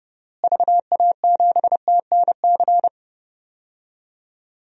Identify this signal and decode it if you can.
Morse code, heard as a keyed tone, about 30 wpm, 700 Hz. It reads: VA7TNC
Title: DXCC-100 Call Signs - 6 Characters - Repeated 3x - 30wpm